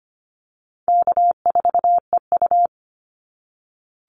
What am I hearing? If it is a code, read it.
Morse code, heard as a keyed tone, about 25 wpm, 700 Hz. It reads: K4EU
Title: Easy Call Signs - 25wpm